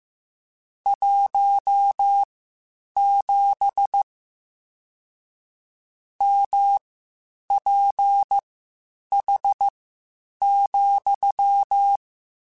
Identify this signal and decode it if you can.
Morse code, heard as a keyed tone, about 15 wpm, spaced as for 8 wpm, 800 Hz. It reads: 17 MPH,
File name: Spaced Characters / Spaced Characters-mp3s / Weather 4 21 sp @13